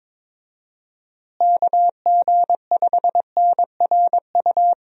Morse code, heard as a keyed tone, about 22 wpm, 700 Hz. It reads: KG5NRU
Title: USA Call Signs - 6 Characters - Repeated 3x - 22wpm